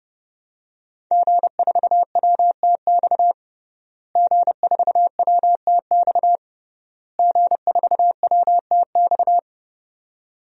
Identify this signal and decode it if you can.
Morse code, heard as a keyed tone, about 30 wpm, 700 Hz. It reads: G4WTX G4WTX G4WTX
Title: DXCC-100 Call Signs - 5 Characters - Repeated 3x - 30wpm